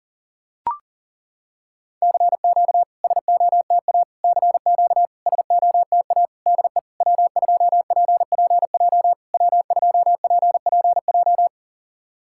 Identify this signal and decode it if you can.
Morse code, heard as a keyed tone, about 40 wpm, 690 Hz. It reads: CQ SOTA CQ SOTA DE W2PPJ W2PPJ
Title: SOTA QSO Practice 40wpm